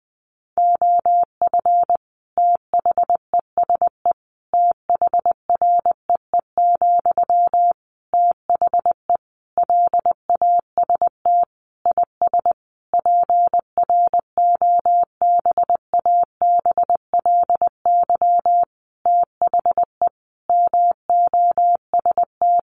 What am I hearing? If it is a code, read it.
Morse code, heard as a keyed tone, about 20 wpm, 700 Hz. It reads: OF THESE THREE, THE LAST IS PROBABLY THE MOST
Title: Sentences from Top 700 Words 20wpm